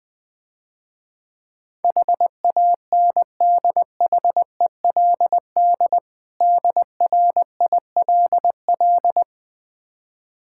Morse code, heard as a keyed tone, about 20 wpm, 700 Hz. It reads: HANDHELD DRILL
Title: Difficult Words - Speed-Racing with Hints - 20wpm